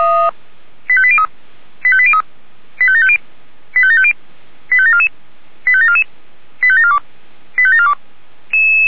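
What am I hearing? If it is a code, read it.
Five-tone selective call (ZVEI), heard as quick runs of five tones, tones of about 70 ms. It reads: 85803 85803 85750 85750 85744 85744 85742 85742
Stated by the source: ZVEI five-tone selective call 85730, received by SDR.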